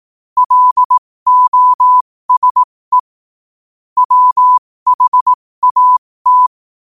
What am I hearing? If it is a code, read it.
Morse code, heard as a keyed tone, about 18 wpm, spaced as for 15 wpm, 1000 Hz. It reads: LOSE WHAT
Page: 15 WPM morse code quotes for Tue, 23 Sep 2025.